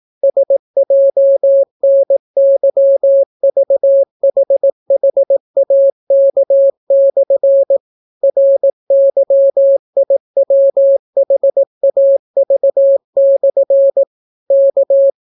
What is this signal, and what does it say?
Morse code, heard as a keyed tone, about 18 wpm, 550 Hz. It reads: SJNYVHHAK/ RYIWHAV/ K